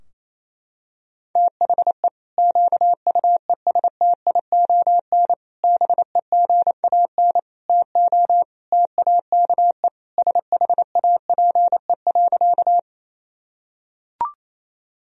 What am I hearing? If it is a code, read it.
Morse code, heard as a keyed tone, about 28 wpm, 700 Hz. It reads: THE QUESTION BEGAN TO TAKE SHAPE.
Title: Sentences from Top 1000 Words - Review Part 3 - 28wpm